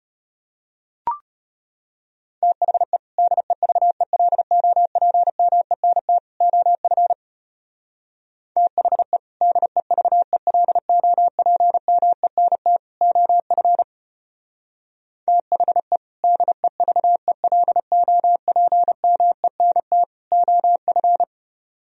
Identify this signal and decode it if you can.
Morse code, heard as a keyed tone, about 31 wpm, 700 Hz. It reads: THE DEVELOPMENT OF THE DEVELOPMENT OF THE DEVELOPMENT OF
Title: Speed-Racing - Sets of 3 Words 25wpm